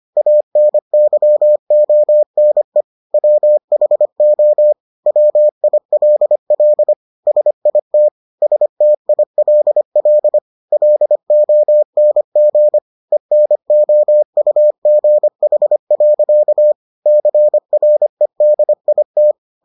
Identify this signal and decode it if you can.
Morse code, heard as a keyed tone, about 25 wpm, 600 Hz. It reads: ANYONE WHO WILL SIT STILL LONG ENOUGH. CREDIT